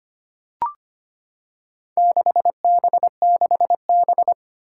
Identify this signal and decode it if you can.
Morse code, heard as a keyed tone, about 25 wpm, 700 Hz. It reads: 6B6B
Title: Four Challenging Letter-Number Combinations 25wpm